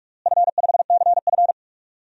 Morse code, heard as a keyed tone, about 46 wpm, 700 Hz. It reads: F5CF
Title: DXCC-100 Call Signs - 4 Characters - Repeated 3x - 45wpm